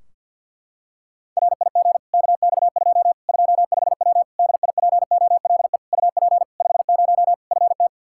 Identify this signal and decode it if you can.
Morse code, heard as a keyed tone, about 47 wpm, 700 Hz. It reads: RIG KX2 25W DIPOLE UP 50 FT
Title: Practice QSO - Exchange 2 - Info - 50wpm